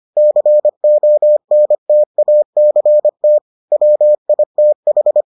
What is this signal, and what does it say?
Morse code, heard as a keyed tone, about 25 wpm, 600 Hz. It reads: CONTACT WITH